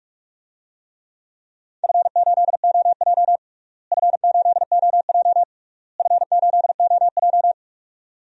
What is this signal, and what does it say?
Morse code, heard as a keyed tone, about 46 wpm, 700 Hz. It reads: F8OJ F8OJ F8OJ
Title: DXCC-100 Call Signs - 4 Characters - Repeated 3x - 45wpm